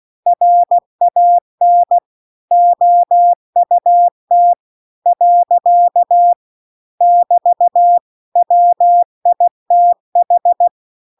Morse code, heard as a keyed tone, about 16 wpm, 700 Hz. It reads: RAN OUT . <BT> WITH
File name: CW-The-Big-Twoheaded-River-2-16WPM